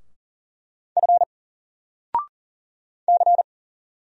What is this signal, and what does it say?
Morse code, heard as a keyed tone, about 40 wpm, 700 Hz. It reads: F C
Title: Single Character (Letter:Number:Punctuation:Prosign) 40wpm